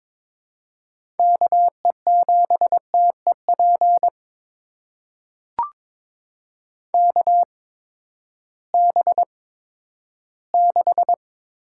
Morse code, heard as a keyed tone, about 22 wpm, spaced as for 13 wpm, 700 Hz. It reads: KE7TEP KB6
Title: USA Call Signs - 6 Characters - 8x Character Spacing - 22wpm